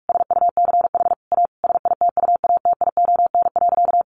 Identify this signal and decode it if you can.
Morse code, heard as a keyed tone, about 44 wpm, 700 Hz. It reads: SUCH A SITUATION.
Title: Sentences from Top 1200 Words 45wpm